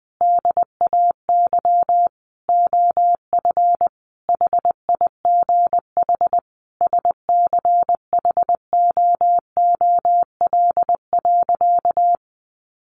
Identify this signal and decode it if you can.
Morse code, heard as a keyed tone, about 20 wpm, 700 Hz. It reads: DAY OF HIGH SCHOOL.
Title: Additional Easy Phrases 20wpm